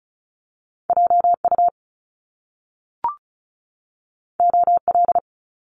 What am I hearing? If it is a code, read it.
Morse code, heard as a keyed tone, about 36 wpm, 700 Hz. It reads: JU OL
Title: Top Two Letter Combinations in Words 35wpm